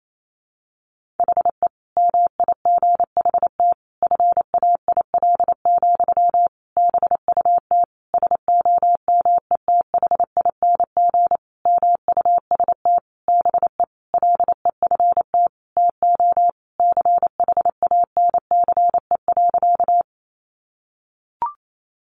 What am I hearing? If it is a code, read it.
Morse code, heard as a keyed tone, about 28 wpm, 700 Hz. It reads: HE MIGHT FAIL, BUT SOMETHING MUST BE LEFT TO CHANCE.
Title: Sentences from Top 1100 Words 28wpm